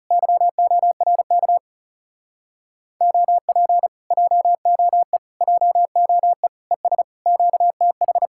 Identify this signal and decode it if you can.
Morse code, heard as a keyed tone, about 40 wpm, 700 Hz. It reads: YORK OP JOE JOE ES QTH